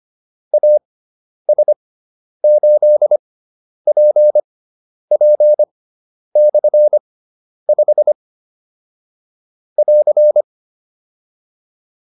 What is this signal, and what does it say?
Morse code, heard as a keyed tone, about 25 wpm, spaced as for 10 wpm, 600 Hz. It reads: AS8PP/5 <AR>